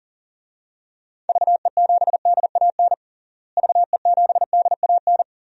Morse code, heard as a keyed tone, about 37 wpm, 700 Hz. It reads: VE7DAN VE7DAN
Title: DXCC-100 Call Signs - 6 Characters - Repeated 2x - 40wpm